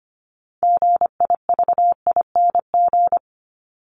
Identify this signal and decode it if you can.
Morse code, heard as a keyed tone, about 25 wpm, 700 Hz. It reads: GIVING
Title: ING Suffix - Top 100 Words with Base Words 25wpm